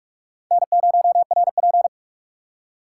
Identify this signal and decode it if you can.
Morse code, heard as a keyed tone, about 46 wpm, 700 Hz. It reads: N0RP
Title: USA Call Signs - 4 Characters - 4x Character Spacing - 45wpm